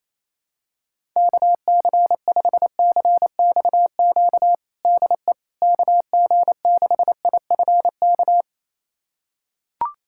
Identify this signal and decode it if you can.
Morse code, heard as a keyed tone, about 28 wpm, 700 Hz. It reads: KC5CXQ DE KG6IFK
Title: USA Callsign DE Callsign 28wpm